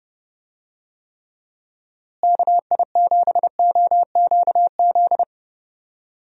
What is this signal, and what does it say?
Morse code, heard as a keyed tone, about 32 wpm, 700 Hz. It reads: KI7OQZ
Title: USA Call Signs - 6 Characters - Repeated 3x - 30wpm